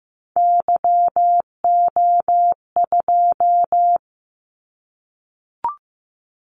Morse code, heard as a keyed tone, about 15 wpm, 700 Hz. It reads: YO2
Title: Callsign Prefixes for Top 25 DXCC - 15wpm